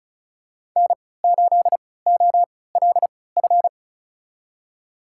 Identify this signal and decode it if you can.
Morse code, heard as a keyed tone, about 34 wpm, spaced as for 20 wpm, 700 Hz. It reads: N8OLF
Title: USA Call Signs - 5 Characters - 3x Character Spacing - 35wpm